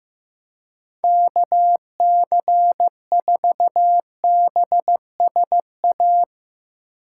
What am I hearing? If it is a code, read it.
Morse code, heard as a keyed tone, about 16 wpm, 700 Hz. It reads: KC4BSA
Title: Speed Racing - USA Call Signs 15wpm